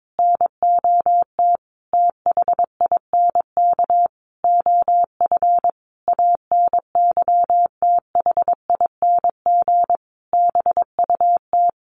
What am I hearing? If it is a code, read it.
Morse code, heard as a keyed tone, about 22 wpm, 700 Hz. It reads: NOT THINK OF ANYTHING BUT
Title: Sentences from Top 1000 Words 22wpm